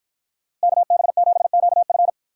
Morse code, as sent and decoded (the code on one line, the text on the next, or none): -.- -... --... --.- ..-.
KB7QF